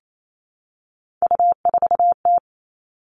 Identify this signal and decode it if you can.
Morse code, heard as a keyed tone, about 28 wpm, 700 Hz. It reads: U4T
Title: All Cut Numbers - 3-Digit 28wpm